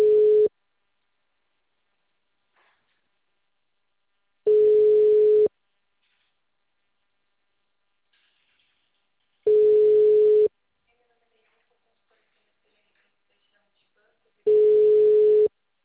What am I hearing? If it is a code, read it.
Ringback tone (Europe, 425 Hz), with speech.